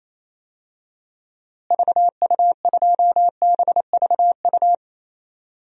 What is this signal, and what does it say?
Morse code, heard as a keyed tone, about 28 wpm, 700 Hz. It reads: VU2BVU
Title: DXCC-100 Call Signs - 6 Characters - Repeated 3x - 28wpm